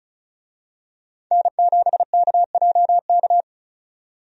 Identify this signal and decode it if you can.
Morse code, heard as a keyed tone, about 36 wpm, 700 Hz. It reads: N7KJK
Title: USA Call Signs - 5 Characters - 35wpm